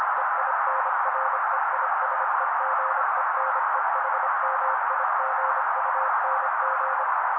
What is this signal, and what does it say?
Morse code, heard as a keyed tone, about 26 wpm, 530 Hz. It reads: EINREISEGENEHMIGUNG
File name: daily / 25wpm noise